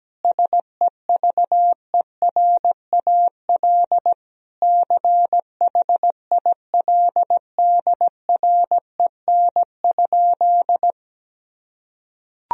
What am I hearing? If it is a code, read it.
Morse code, heard as a keyed tone, about 17 wpm, 700 Hz. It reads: SEVERAL CHILDREN?
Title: Sentences from Top 1000 Words - Review Part 2 - 17wpm